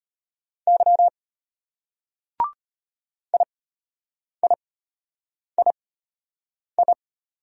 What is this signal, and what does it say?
Morse code, heard as a keyed tone, about 34 wpm, spaced as for 8 wpm, 700 Hz. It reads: Y IIII